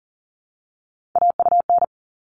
Morse code, heard as a keyed tone, about 40 wpm, 700 Hz. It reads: AUN
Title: All Cut Numbers - 3-Digit 40wpm